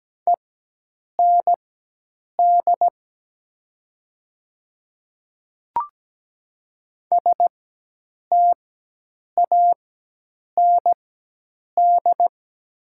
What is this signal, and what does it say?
Morse code, heard as a keyed tone, about 17 wpm, spaced as for 8 wpm, 700 Hz. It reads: END STAND
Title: 4x Character Spacing - Top 300 Words - 17wpm